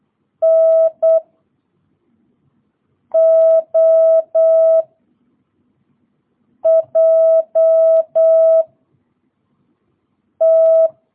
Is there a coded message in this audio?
Yes: Morse code from 0.4 to 10.9 s, heard as a keyed tone, about 8 wpm, 660 Hz. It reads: N O J T